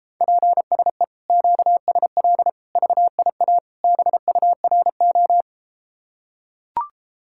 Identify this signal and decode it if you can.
Morse code, heard as a keyed tone, about 33 wpm, 700 Hz. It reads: PSE QSL VIA BURO